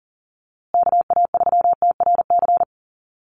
Morse code, heard as a keyed tone, about 40 wpm, 700 Hz. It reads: KA3TRC
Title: USA Call Signs Repeated 3x 40wpm